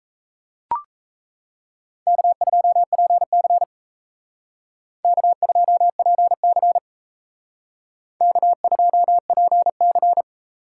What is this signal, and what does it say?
Morse code, heard as a keyed tone, about 32 wpm, 700 Hz. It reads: K2PC K2PC K2PC
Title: Speed Racing - USA 1x2 Call Signs 28wpm